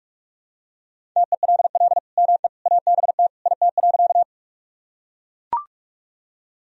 Morse code, heard as a keyed tone, about 41 wpm, 700 Hz. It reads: TELL ME ABT IT.
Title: Sentences from Top 200 Words and Abbreviations 45wpm